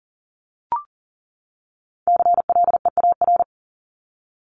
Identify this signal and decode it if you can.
Morse code, heard as a keyed tone, about 39 wpm, 700 Hz. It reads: CLEAR
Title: LY Suffix - Top 100 Words with Base Words 40wpm